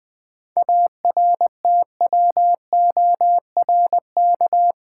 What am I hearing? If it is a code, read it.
Morse code, heard as a keyed tone, about 20 wpm, 700 Hz. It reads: ARTWORK
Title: 7 Letter Words - First 3 Letters Given - 20wpm